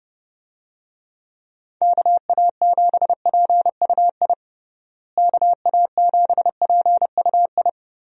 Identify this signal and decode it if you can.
Morse code, heard as a keyed tone, about 30 wpm, 700 Hz. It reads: KA7PUI KA7PUI
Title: USA Call Signs - 6 Characters - Repeated 3x - 30wpm